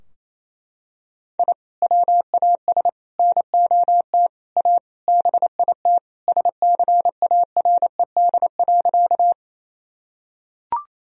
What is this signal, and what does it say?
Morse code, heard as a keyed tone, about 28 wpm, 700 Hz. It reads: I WAS NOT A BIT SCARED.
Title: Sentences from Top 1000 Words - Review Part 4 - 28wpm